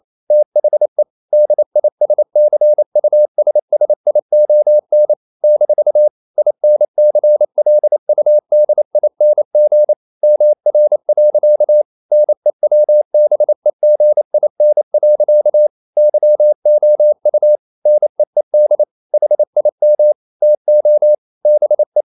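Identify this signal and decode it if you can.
Morse code, heard as a keyed tone, about 28 wpm, 600 Hz. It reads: THE DISCUSSION - INCLUDING MR. NEWBEGIN. YOU NEED HIM TO BE
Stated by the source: Morse code practice at 28 wpm, text of Ken Follett's novel Never in English, from CW med Gnister.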